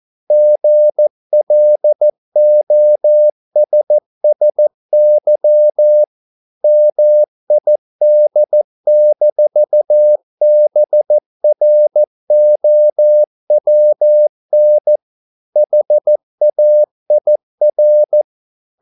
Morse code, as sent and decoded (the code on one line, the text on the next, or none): --. .-.. --- ... ... -.-- / -- .. -.. -....- -... .-. --- .-- -. / .... .- .. .-.
GLOSSY MID-BROWN HAIR